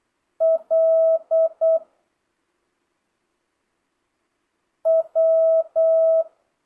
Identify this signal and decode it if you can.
Morse code, heard as a keyed tone, about 8 wpm, 650 Hz. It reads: L W
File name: jh8cbh / 2025cw